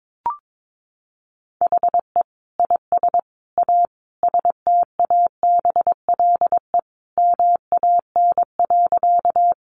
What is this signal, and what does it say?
Morse code, heard as a keyed tone, about 22 wpm, 700 Hz. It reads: HE IS A STABLE MAN.
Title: Sentences from Top 1500 Words 22wpm